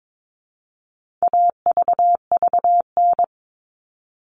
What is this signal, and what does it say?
Morse code, heard as a keyed tone, about 22 wpm, 700 Hz. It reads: AVVN
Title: All Cut Numbers - 4-Digit 22wpm